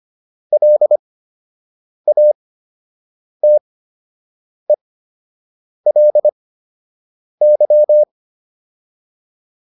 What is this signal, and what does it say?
Morse code, heard as a keyed tone, about 25 wpm, spaced as for 7 wpm, 600 Hz. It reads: LATELY